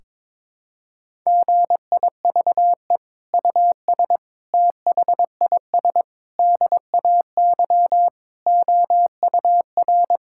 Morse code, heard as a keyed tone, about 22 wpm, 700 Hz. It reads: GIVE US THIS DAY OUR
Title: Sentences from Top 1600 Words 22wpm